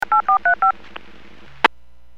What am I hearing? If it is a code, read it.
DTMF, heard as telephone touch tones, tones of about 110 ms. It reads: 8435
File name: WPS / 00441 13-37 WPS 451.25 WPS